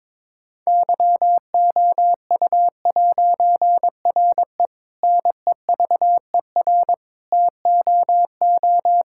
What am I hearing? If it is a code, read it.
Morse code, heard as a keyed tone, about 22 wpm, 700 Hz. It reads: YOU'RE NEVER TOO
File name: Difficult Sentences with Hints - 22wpm